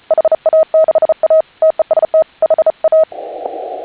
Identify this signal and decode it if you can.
Morse code, heard as a keyed tone, about 34 wpm, 630 Hz. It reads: HA6A TEST HA
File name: HA6A CW